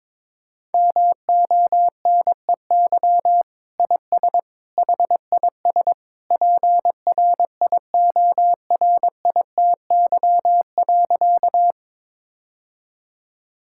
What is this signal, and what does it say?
Morse code, heard as a keyed tone, about 21 wpm, 700 Hz. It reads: MONEY IS HIS PRIORITY.
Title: Sentences from Top 1600 Words 22wpm